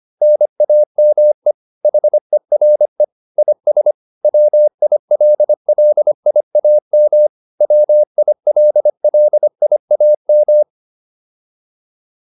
Morse code, heard as a keyed tone, about 25 wpm, 600 Hz. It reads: NAME HERE IS WILLIAM WILLIAM
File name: QSO307a 25